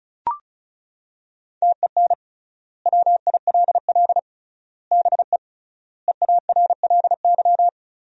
Morse code, heard as a keyed tone, about 36 wpm, 700 Hz. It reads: TEN WILL BE EARLY
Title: 3x Word-Spacing - Sentences from Top 400 Words - 35wpm